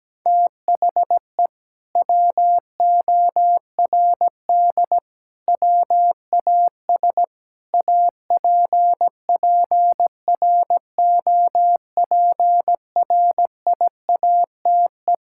Morse code, heard as a keyed tone, about 17 wpm, 700 Hz. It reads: THE WORD WAS APPROPRIATE
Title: Sentences from Top 1000 Words - Review Part 5 - 17wpm